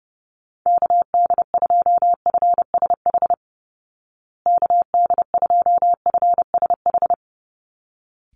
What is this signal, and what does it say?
Morse code, heard as a keyed tone, about 30 wpm, 700 Hz. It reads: KD2FSH KD2FSH
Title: Call Signs Repeated 3x - 30wpm